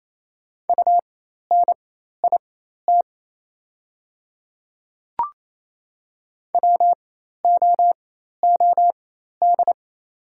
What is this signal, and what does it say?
Morse code, heard as a keyed tone, about 28 wpm, spaced as for 13 wpm, 690 Hz. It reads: UNIT WOOD
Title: 4x Character Spacing - Top 500 Words - 28wpm